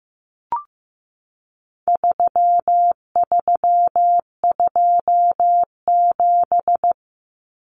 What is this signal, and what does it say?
Morse code, heard as a keyed tone, about 15 wpm, 700 Hz. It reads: 3327
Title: Common Cut Numbers - 4-Digit 15wpm